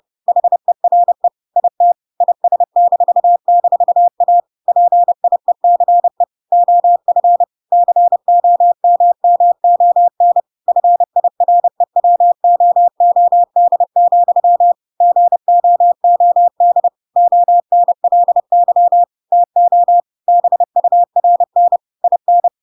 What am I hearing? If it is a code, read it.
Morse code, heard as a keyed tone, about 30 wpm, 700 Hz. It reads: HERE IT IS--A PIECE OF COMMON FIREWOOD, GOOD ONLY TO BURN IN
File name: Pinocchio / Pinocchio ch1 30 30 ik1hge